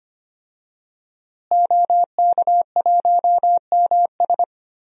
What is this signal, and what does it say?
Morse code, heard as a keyed tone, about 25 wpm, 700 Hz. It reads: OK1MS
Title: DXCC-100 Call Signs - 5 Characters - Repeated 2x - 25wpm